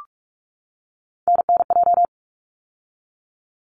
Morse code, heard as a keyed tone, about 46 wpm, 700 Hz. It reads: NNJ
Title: ARRL Sections 45wpm